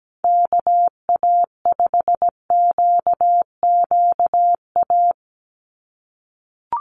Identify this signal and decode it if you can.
Morse code, heard as a keyed tone, about 17 wpm, 700 Hz. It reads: KA5QQA
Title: NA SOTA Chaser Callsigns 17wpm